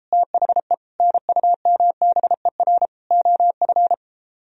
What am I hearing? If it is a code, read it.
Morse code, heard as a keyed tone, about 33 wpm, 700 Hz. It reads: THE NUMBER OF